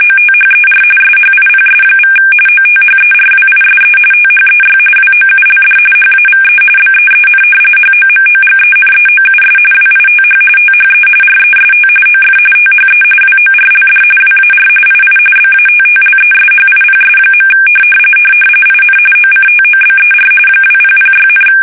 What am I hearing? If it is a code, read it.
RTTY, about 45.45 baud. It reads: OTHER SHIPS= OTHERWISE THEY WOULD HAVE BEEN HOMOGENEOUS WITH THE REST OF THE WORLD WAR II BATTLESHIPS. IN OCTOBER 2006,
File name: fsk850 / battleships